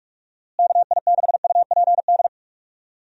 Morse code, heard as a keyed tone, about 44 wpm, 700 Hz. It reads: KI6UPD
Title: USA Call Signs - 6 Characters - Repeated 3x - 45wpm